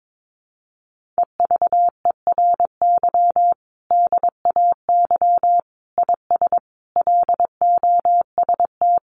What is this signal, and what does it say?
Morse code, heard as a keyed tone, about 22 wpm, 700 Hz. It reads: EVERY DAY IS LOST
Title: Sentences from Top 1400 Words 22wpm